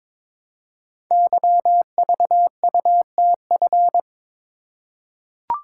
Morse code, heard as a keyed tone, about 22 wpm, 700 Hz. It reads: YVUTF
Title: Code Groups - 5 Letter:Numbers - 22wpm